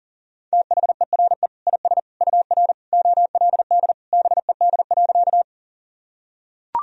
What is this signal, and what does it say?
Morse code, heard as a keyed tone, about 37 wpm, 700 Hz. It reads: THERE IS UR OLD BED.
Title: Sentences from Top 500 Words and Abbreviations 40wpm